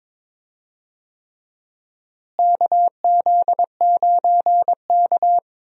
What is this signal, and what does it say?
Morse code, heard as a keyed tone, about 22 wpm, 700 Hz. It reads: KZ9K
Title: USA Call Signs - 4 Characters - Repeated 3x - 22wpm